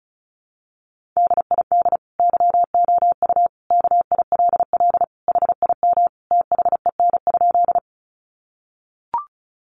Morse code, heard as a keyed tone, about 34 wpm, 700 Hz. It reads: DID YOU KILL HIM THEN?
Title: Sentences from Top 1400 Words 35wpm